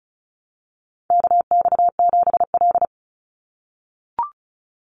Morse code, heard as a keyed tone, about 34 wpm, 700 Hz. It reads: KX7L
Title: NA SOTA Chaser Callsigns 35wpm